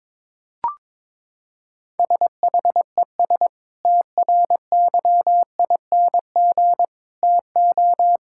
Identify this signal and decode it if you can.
Morse code, heard as a keyed tone, about 22 wpm, 700 Hz. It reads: SHES TRYING TO
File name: practice / 22wpm / Sentences with 2-Word Phrases - 22wpm